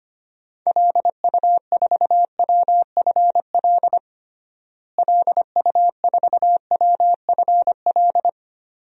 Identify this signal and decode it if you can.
Morse code, heard as a keyed tone, about 25 wpm, 700 Hz. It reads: LU4WFL LU4WFL
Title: DXCC-100 Call Signs - 6 Characters - Repeated 2x - 25wpm